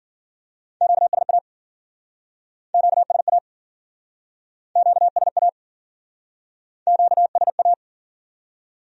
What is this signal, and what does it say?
Morse code, heard as a keyed tone, about 55 wpm, 700 Hz. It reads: QSA QSA QSA QSA